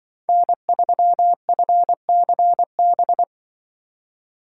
Morse code, heard as a keyed tone, about 25 wpm, 700 Hz. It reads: N3FCB
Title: Speed Racing - USA 1x3 Call Signs 20wpm